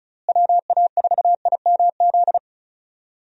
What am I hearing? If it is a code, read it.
Morse code, heard as a keyed tone, about 34 wpm, 700 Hz. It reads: WA4IMZ
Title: USA Call Signs - 6 Characters - Repeated 3x - 35wpm